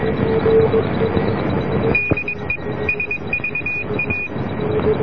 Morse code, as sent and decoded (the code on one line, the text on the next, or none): -.. / ... ...- .-
DE SVA